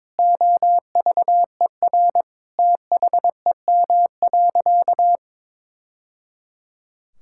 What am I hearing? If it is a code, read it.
Morse code, heard as a keyed tone, about 22 wpm, 700 Hz. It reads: OVER THEM.
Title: Sentences from Top 1000 Words - Review Part 6 - 22wpm